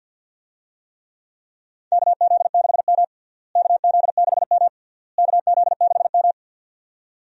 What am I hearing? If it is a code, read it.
Morse code, heard as a keyed tone, about 48 wpm, 700 Hz. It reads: KZ6M KZ6M KZ6M
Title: USA Call Signs - 4 Characters - Repeated 3x - 50wpm